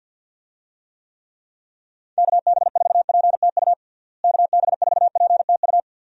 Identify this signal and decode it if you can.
Morse code, heard as a keyed tone, about 46 wpm, 700 Hz. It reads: KB4PTU KB4PTU
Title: USA Call Signs - 6 Characters - Repeated 2x - 50wpm